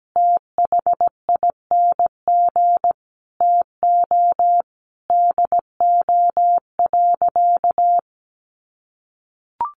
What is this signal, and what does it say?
Morse code, heard as a keyed tone, about 17 wpm, 700 Hz. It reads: THING TO DO.
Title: Sentences from Top 1300 Words 17wpm